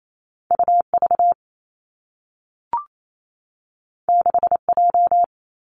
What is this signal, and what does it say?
Morse code, heard as a keyed tone, about 28 wpm, 700 Hz. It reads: U V 6 J
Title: Two Challenging Letter-Number Combinations 28wpm